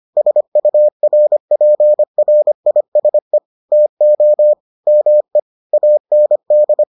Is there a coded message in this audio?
Yes: Morse code from 0.2 to 6.8 s, heard as a keyed tone, about 25 wpm, 600 Hz. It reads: SURPRISE TO ME AND